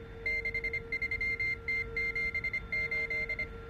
Morse code, heard as a keyed tone, about 25 wpm, 2100 Hz, with noise. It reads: 63T78